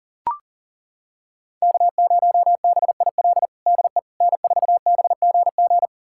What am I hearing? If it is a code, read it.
Morse code, heard as a keyed tone, about 40 wpm, 700 Hz. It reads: K0BIL DE N4BGG
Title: USA Callsign DE Callsign - 5 Characters - 40wpm